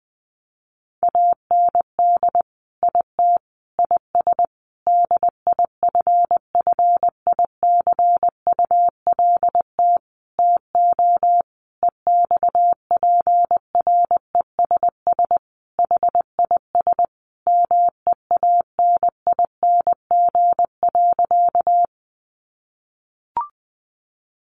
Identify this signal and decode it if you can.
Morse code, heard as a keyed tone, about 20 wpm, 700 Hz. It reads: AND IT IS DIFFICULT TO EXPRESS HIS MEANING.
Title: Sentences from Top 1100 Words 20wpm